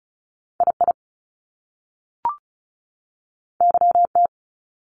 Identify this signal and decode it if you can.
Morse code, heard as a keyed tone, about 36 wpm, 700 Hz. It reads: I I Y T
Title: Top Two Letter Combinations in Words 35wpm